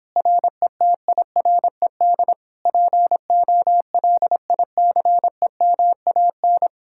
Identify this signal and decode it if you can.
Morse code, heard as a keyed tone, about 29 wpm, 700 Hz. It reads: RETIRED POLICEMAN